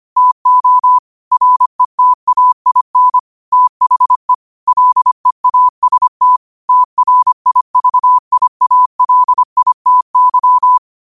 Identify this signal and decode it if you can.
Morse code, heard as a keyed tone, about 25 wpm, 1000 Hz. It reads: TO RETAIN THE LEAST TRIVIALITY
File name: qotdfiles / 25wpm